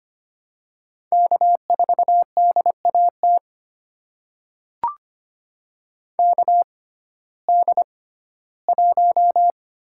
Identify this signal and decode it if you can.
Morse code, heard as a keyed tone, about 25 wpm, spaced as for 15 wpm, 700 Hz. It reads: K4DAT KD1
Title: USA Call Signs - 5 Characters - 6x Character Spacing - 25wpm